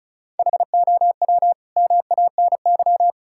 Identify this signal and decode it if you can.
Morse code, heard as a keyed tone, about 34 wpm, 700 Hz. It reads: HOW MANY
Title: Sentences from Top 1000 Words - Review Part 5 - 35wpm